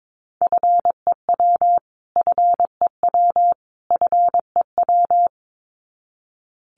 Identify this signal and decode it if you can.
Morse code, heard as a keyed tone, about 22 wpm, 700 Hz. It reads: FEW FEW FEW
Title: Top 500 Words Repeated 22wpm